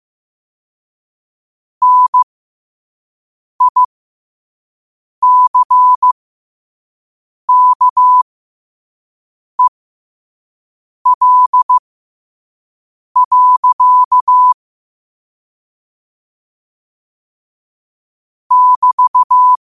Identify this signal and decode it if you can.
Morse code, heard as a keyed tone, about 15 wpm, spaced as for 5 wpm, 1000 Hz. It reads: NICKEL. <BT>
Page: New quotes every day in morse code at 5 Words per minute.